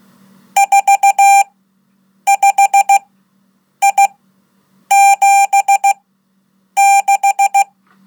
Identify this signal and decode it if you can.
Morse code, heard as a keyed tone, about 16 wpm, spaced as for 8 wpm, 790 Hz. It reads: 45I76
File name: Code / 5min-1